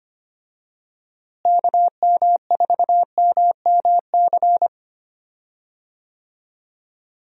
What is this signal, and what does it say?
Morse code, heard as a keyed tone, about 25 wpm, 700 Hz. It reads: KM4MMC
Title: USA Call Signs - 6 Characters - Repeated 2x - 25wpm